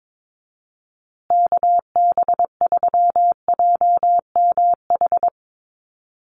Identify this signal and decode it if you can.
Morse code, heard as a keyed tone, about 22 wpm, 700 Hz. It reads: KB3JMH
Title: USA Call Signs Repeated 3x 22wpm